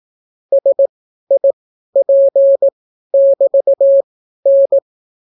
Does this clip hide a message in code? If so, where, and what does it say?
Yes: Morse code from 0.5 to 4.8 s, heard as a keyed tone, about 18 wpm, spaced as for 12 wpm, 550 Hz. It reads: SIP<BT>N